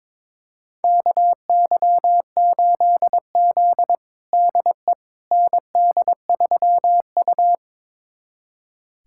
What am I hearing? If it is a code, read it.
Morse code, heard as a keyed tone, about 21 wpm, 700 Hz. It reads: KY8Z DE ND3U
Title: USA Callsign DE Callsign - 4 Characters - 22wpm